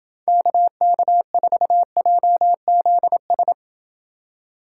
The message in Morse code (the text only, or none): KK4JZS